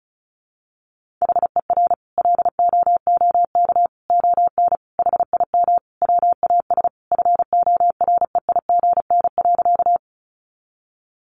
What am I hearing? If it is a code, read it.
Morse code, heard as a keyed tone, about 34 wpm, 700 Hz. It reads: HER LOOK ON HIM WAS FOREIGN.
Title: Sentences from Top 1400 Words 35wpm